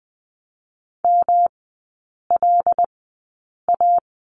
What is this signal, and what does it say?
Morse code, heard as a keyed tone, about 19 wpm, spaced as for 8 wpm, 700 Hz. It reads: MLA
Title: 166 Lesson 27e - Number 7 - Three Characters 20wpm